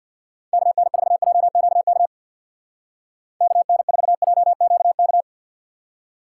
Morse code, heard as a keyed tone, about 60 wpm, 700 Hz. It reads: KN4JQK KN4JQK
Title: Speed Racing - USA 2x3 Call Signs 50wpm